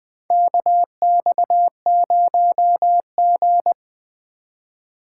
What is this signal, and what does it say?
Morse code, heard as a keyed tone, about 20 wpm, 700 Hz. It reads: KX0G
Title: USA Call Signs - 4 Characters - 2x Character Spacing - 20wpm